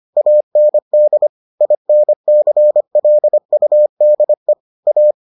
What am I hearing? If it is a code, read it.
Morse code, heard as a keyed tone, about 25 wpm, 600 Hz. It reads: AND INCLUDE A